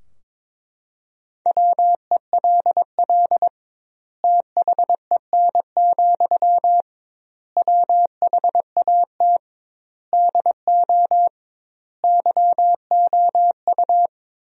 Morse code, heard as a keyed tone, about 21 wpm, 700 Hz. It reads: WELL THEN, WHAT DO YOU
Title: 2x Word-Spacing - Sentences from Top 200 Words - 22wpm